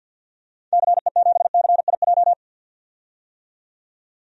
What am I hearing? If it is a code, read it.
Morse code, heard as a keyed tone, about 53 wpm, 700 Hz. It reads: CE7CIJ
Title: DXCC-100 Call Signs - 6 Characters - 50wpm